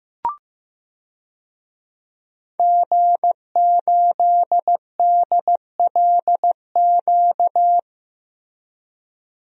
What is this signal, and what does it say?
Morse code, heard as a keyed tone, about 15 wpm, 700 Hz. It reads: G8DLQ
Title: DXCC-100 Call Signs - 5 Characters - Repeated 2x - 15wpm